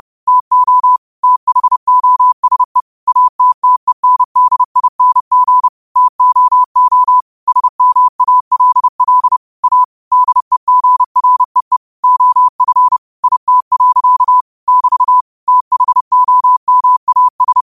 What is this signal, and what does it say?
Morse code, heard as a keyed tone, about 30 wpm, 1000 Hz. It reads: TO THOSE ATTENDING TOO SMALL A DEGREE OF IT. <BT> THOMAS
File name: qotdfiles / 30wpm